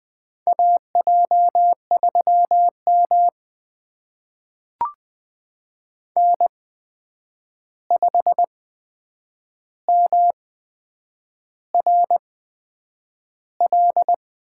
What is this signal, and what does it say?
Morse code, heard as a keyed tone, about 20 wpm, spaced as for 8 wpm, 700 Hz. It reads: AJ3M N5MRL
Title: USA Call Signs - 8x Character Spacing - 20wpm